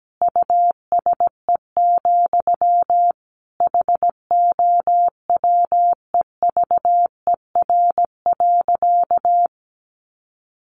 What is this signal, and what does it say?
Morse code, heard as a keyed tone, about 17 wpm, 700 Hz. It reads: USE, HOWEVER.
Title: Sentences from Top 1400 Words 17wpm